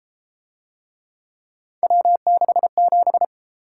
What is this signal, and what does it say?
Morse code, heard as a keyed tone, about 31 wpm, 700 Hz. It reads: W67